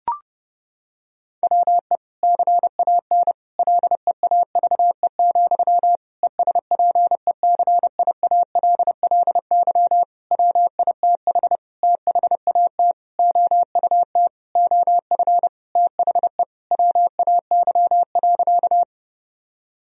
Morse code, heard as a keyed tone, about 30 wpm, 700 Hz. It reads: WE CAN LEAVE, ESPECIALLY WITH THAT OUT OF THE WAY.
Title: Sentences from Top 1000 Words - Review Part 4 - 30wpm